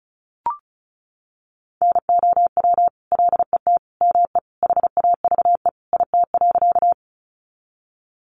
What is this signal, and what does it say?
Morse code, heard as a keyed tone, about 34 wpm, 700 Hz. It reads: NOW LET ME HAVE IT.
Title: Sentences from Top 300 Words 35wpm